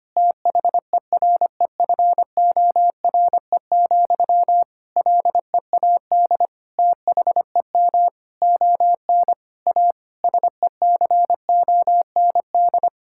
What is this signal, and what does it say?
Morse code, heard as a keyed tone, about 25 wpm, 700 Hz. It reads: THEREFORE, LEAD THEM ON A SECOND
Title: Sentences from Top 1000 Words - Review Part 5 - 25wpm